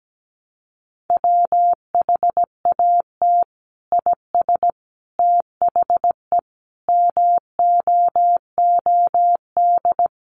Morse code, heard as a keyed tone, about 17 wpm, 700 Hz. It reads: WHAT IS THE MOOD
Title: Sentences from Top 1100 Words 17wpm